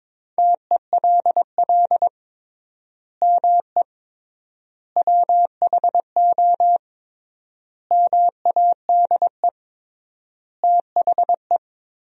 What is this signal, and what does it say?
Morse code, heard as a keyed tone, about 21 wpm, 700 Hz. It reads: TELL ME WHO MADE THE
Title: 3x Word-Spacing - Sentences from Top 200 Words - 22wpm